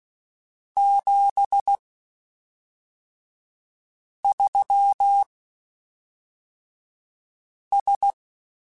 Morse code, heard as a keyed tone, about 16 wpm, spaced as for 3 wpm, 800 Hz. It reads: 73S
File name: (spaced) / QSOs Spaced-mp3s / kc7sac @5 sp